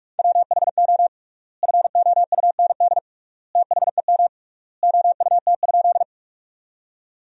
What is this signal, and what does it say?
Morse code, heard as a keyed tone, about 47 wpm, 700 Hz. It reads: WHO FOUND THEM OUT?
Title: 3x Word-Spacing - Sentences from Top 300 Words - 45wpm